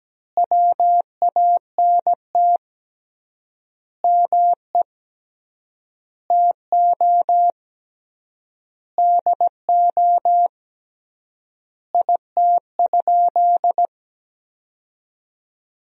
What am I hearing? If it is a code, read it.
Morse code, heard as a keyed tone, about 17 wpm, 700 Hz. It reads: WANT ME TO DO IT?
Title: 3x Word-Spacing - Sentences from Top 200 Words - 17wpm